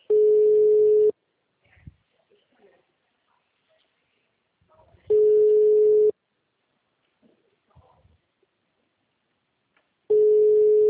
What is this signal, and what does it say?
Ringback tone (Europe, 425 Hz), with speech.